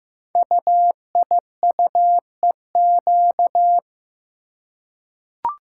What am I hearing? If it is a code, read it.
Morse code, heard as a keyed tone, about 15 wpm, 700 Hz. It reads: UIUEQ
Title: Code Groups - 5 Letters - 15wpm